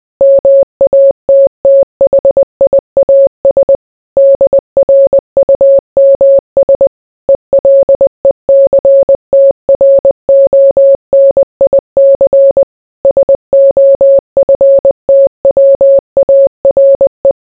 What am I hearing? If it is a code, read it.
Morse code, heard as a keyed tone, about 20 wpm, 550 Hz. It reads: MATTHIAS DRUMS ELECTRONIC SOFTWARE